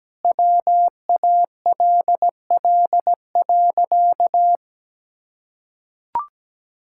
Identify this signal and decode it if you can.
Morse code, heard as a keyed tone, about 17 wpm, 700 Hz. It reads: WALL.
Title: Sentences from Top 1000 Words - Review Part 6 - 17wpm